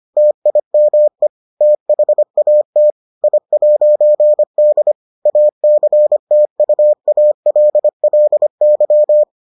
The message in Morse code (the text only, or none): TIME THAT I'D ACTUALLY